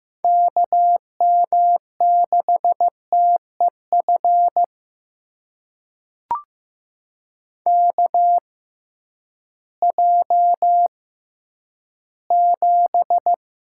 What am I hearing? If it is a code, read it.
Morse code, heard as a keyed tone, about 15 wpm, spaced as for 10 wpm, 700 Hz. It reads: KM6TEF KJ7
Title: USA Call Signs - 6 Characters - 6x Character Spacing - 15wpm